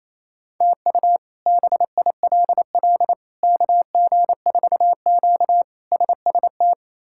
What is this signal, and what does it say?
Morse code, heard as a keyed tone, about 27 wpm, 700 Hz. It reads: TU BILL KG4Q SST
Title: K1USN SST QSO Practice - 28wpm